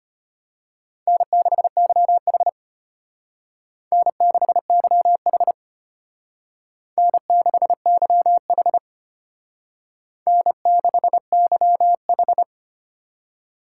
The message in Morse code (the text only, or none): N6YH N6YH N6YH N6YH